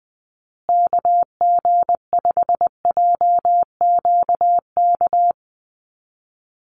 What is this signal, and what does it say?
Morse code, heard as a keyed tone, about 20 wpm, 700 Hz. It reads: KG5JQK
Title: USA Call Signs Repeated 2x 20wpm